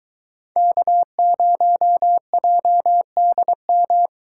Morse code, heard as a keyed tone, about 24 wpm, 700 Hz. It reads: K0JDM